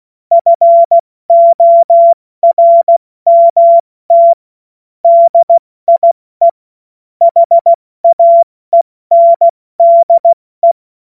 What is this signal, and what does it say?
Morse code, heard as a keyed tone, about 16 wpm, 680 Hz. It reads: FORMT DIE HAENDE
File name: Morsekurs / Witze / Witz 16. August 70 F80